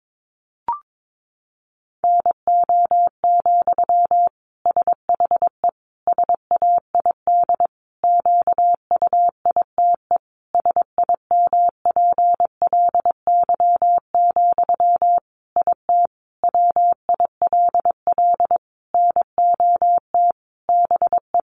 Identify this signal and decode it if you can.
Morse code, heard as a keyed tone, about 22 wpm, 700 Hz. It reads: NO, SHE SAID QUITE SIMPLY, IT WILL NOT BE
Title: Sentences from Top 800 Words 22wpm